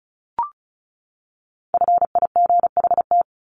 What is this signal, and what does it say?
Morse code, heard as a keyed tone, about 34 wpm, 700 Hz. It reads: FIGHT
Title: Top 1501-1750 Words 35wpm